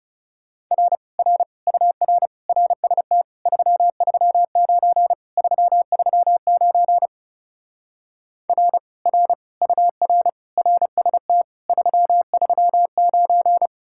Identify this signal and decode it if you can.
Morse code, heard as a keyed tone, about 38 wpm, 700 Hz. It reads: R R UR RST 339 339 R R UR RST 339